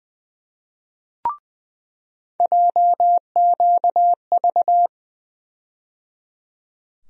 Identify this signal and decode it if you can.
Morse code, heard as a keyed tone, about 20 wpm, 700 Hz. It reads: JQV
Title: USA Call Sign Postfixes - 3 Letters 20wpm